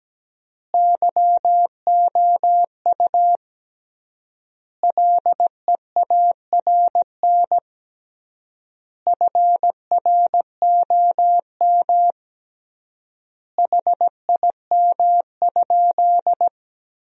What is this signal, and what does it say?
Morse code, heard as a keyed tone, about 17 wpm, 700 Hz. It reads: YOU LEARN FROM HIM?
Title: 3x Word-Spacing - Sentences from Top 300 Words - 17wpm